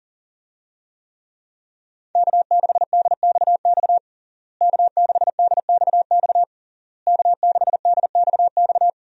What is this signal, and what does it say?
Morse code, heard as a keyed tone, about 40 wpm, 700 Hz. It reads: K6DXX K6DXX K6DXX
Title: USA Call Signs - 5 Characters - Repeated 3x - 40wpm